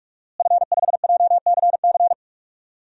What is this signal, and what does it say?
Morse code, heard as a keyed tone, about 46 wpm, 700 Hz. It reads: F5JCC
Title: DXCC-100 Call Signs - 5 Characters - Repeated 3x - 45wpm